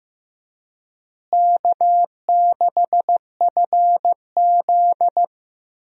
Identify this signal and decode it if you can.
Morse code, heard as a keyed tone, about 15 wpm, 700 Hz. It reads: K6FZ
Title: USA Call Signs - 4 Characters - Repeated 2x - 15wpm